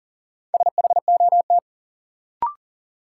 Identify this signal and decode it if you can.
Morse code, heard as a keyed tone, about 38 wpm, 700 Hz. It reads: SHOT
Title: Sentences from Top 1100 Words - New Word Drill Alt - 40wpm